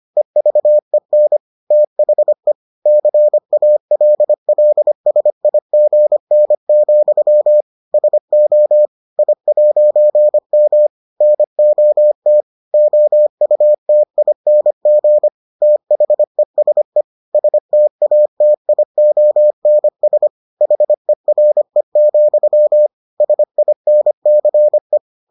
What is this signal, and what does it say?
Morse code, heard as a keyed tone, about 25 wpm, 600 Hz. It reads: EVEN THE CALLSIGN, SO I'M NOT OUTING THESE STATIONS HERE, SINCE